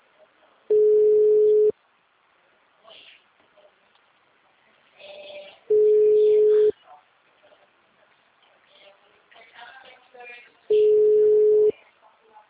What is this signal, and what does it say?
Ringback tone (Europe, 425 Hz), with speech.
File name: out- -2005- - - .30890